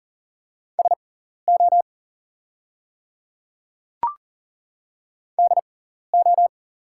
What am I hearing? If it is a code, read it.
Morse code, heard as a keyed tone, about 42 wpm, spaced as for 14 wpm, 700 Hz. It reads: SO DO
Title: 6x Character Spacing - Top 100 Words - 40wpm